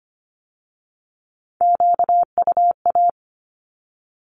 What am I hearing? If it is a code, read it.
Morse code, heard as a keyed tone, about 25 wpm, 700 Hz. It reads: QUA
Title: Q-Codes with Spoken Meaning - 25wpm